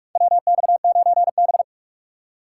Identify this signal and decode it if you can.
Morse code, heard as a keyed tone, about 46 wpm, 700 Hz. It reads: WX9B
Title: USA Call Signs - 4 Characters - 3x Character Spacing - 45wpm